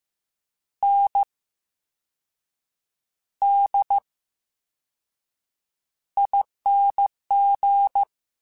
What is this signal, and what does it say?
Morse code, heard as a keyed tone, about 15 wpm, spaced as for 6 wpm, 800 Hz. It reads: NDING
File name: Special INGs-mp3s / Special INGs-mp3s / VerySlowToFaster Ing @13